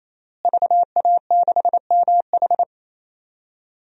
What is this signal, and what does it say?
Morse code, heard as a keyed tone, about 28 wpm, 690 Hz. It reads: VA6MH
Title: DXCC-100 Call Signs - 5 Characters - Repeated 3x - 28wpm